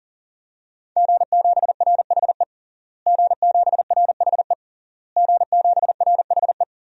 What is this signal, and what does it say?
Morse code, heard as a keyed tone, about 40 wpm, 700 Hz. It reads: G7RHE G7RHE G7RHE
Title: DXCC-100 Call Signs - 5 Characters - Repeated 3x - 40wpm